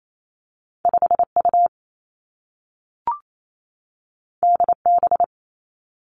Morse code, heard as a keyed tone, about 28 wpm, 700 Hz. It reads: 5 U D B
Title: Two Challenging Letter-Number Combinations 28wpm